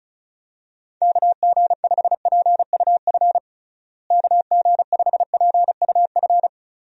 Morse code, heard as a keyed tone, about 36 wpm, 700 Hz. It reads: KG5PUF KG5PUF
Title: USA Call Signs - 6 Characters - Repeated 2x - 35wpm